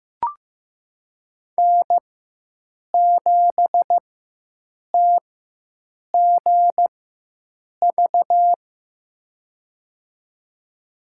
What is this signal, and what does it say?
Morse code, heard as a keyed tone, about 15 wpm, spaced as for 7 wpm, 700 Hz. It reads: N7TGV
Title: USA Call Signs - 4x Character Spacing - 15wpm